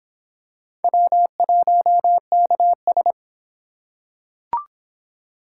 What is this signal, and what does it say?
Morse code, heard as a keyed tone, about 24 wpm, 700 Hz. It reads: W1KS